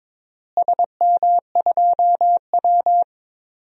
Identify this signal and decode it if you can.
Morse code, heard as a keyed tone, about 22 wpm, 700 Hz. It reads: SM2W
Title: DXCC-100 Call Signs - 4 Characters - Repeated 3x - 22wpm